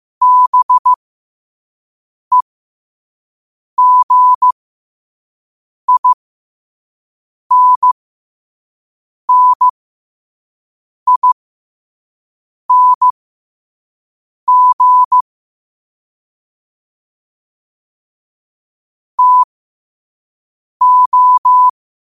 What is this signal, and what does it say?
Morse code, heard as a keyed tone, about 15 wpm, spaced as for 5 wpm, 1000 Hz. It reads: BEGINNING TO
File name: qotdfiles / 5wpm